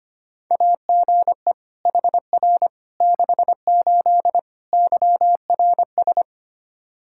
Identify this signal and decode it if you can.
Morse code, heard as a keyed tone, about 25 wpm, 700 Hz. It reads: AGE HR 68 YRS
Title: Practice QSO - Exchange 3 - More Info - 25wpm